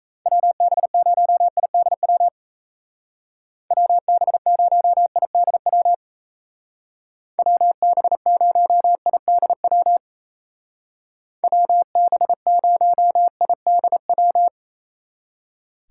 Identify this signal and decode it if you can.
Morse code, heard as a keyed tone, about 34 wpm, 700 Hz. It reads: WB0IDW WB0IDW WB0IDW WB0IDW
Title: Speed Racing - USA 2x3 Call Signs 28wpm